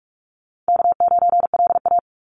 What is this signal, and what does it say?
Morse code, heard as a keyed tone, about 44 wpm, 700 Hz. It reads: K9LA
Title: Easy Call Signs Repeated 2x - 45wpm